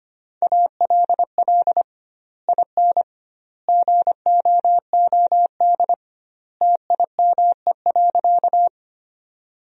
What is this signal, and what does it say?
Morse code, heard as a keyed tone, about 25 wpm, 700 Hz. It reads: ALL IN GOOD TIME.
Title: 2x Word-Spacing - Sentences from Top 200 Words - 25wpm